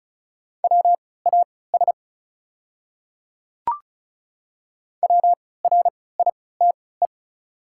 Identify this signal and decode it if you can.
Morse code, heard as a keyed tone, about 34 wpm, spaced as for 20 wpm, 690 Hz. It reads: WAS WRITE
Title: 3x Character Spacing - Top 100 Words - 35wpm